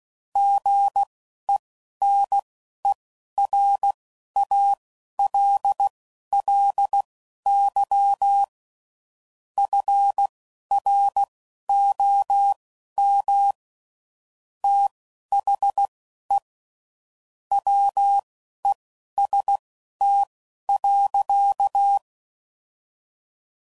Morse code, heard as a keyed tone, about 16 wpm, spaced as for 11 wpm, 800 Hz. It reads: GENERALLY FROM THE WEST.
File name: CW Code Groups @10WPM / Weather Sets / Weather 5 1